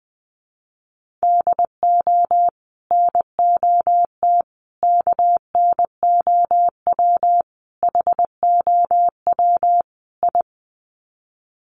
Morse code, heard as a keyed tone, about 20 wpm, 700 Hz. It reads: DO NOT KNOW HOW I
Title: Sets of 5 Words Encore2 20wpm